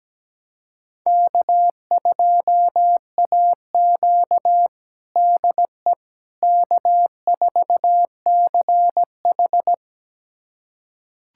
Morse code, heard as a keyed tone, about 17 wpm, 700 Hz. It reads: K2AQ DE K4CH
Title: USA Callsign DE Callsign - 4 Characters - 17wpm